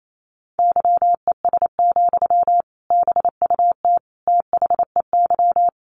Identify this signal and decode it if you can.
Morse code, heard as a keyed tone, about 28 wpm, 700 Hz. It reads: YES, BUT THEY
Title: Sentences from Top 1400 Words 28wpm